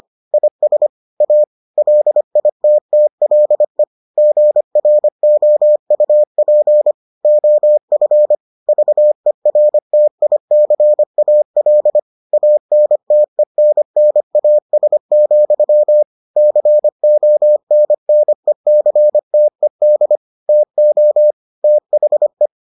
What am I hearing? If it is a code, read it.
Morse code, heard as a keyed tone, about 25 wpm, 600 Hz. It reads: IS A LITTLE GROUP OF VERTICAL ANTENNAS, CONNECTED TO THE